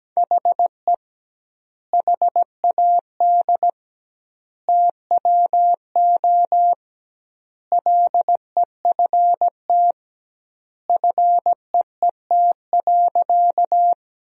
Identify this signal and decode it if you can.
Morse code, heard as a keyed tone, about 17 wpm, 700 Hz. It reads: HE HAD TWO LEFT FEET.
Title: 2x Word-Spacing - Sentences from Top 300 Words - 17wpm